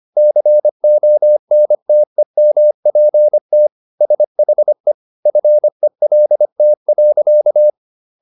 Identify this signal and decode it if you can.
Morse code, heard as a keyed tone, about 25 wpm, 600 Hz. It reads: CONTEMPT SHE FELT.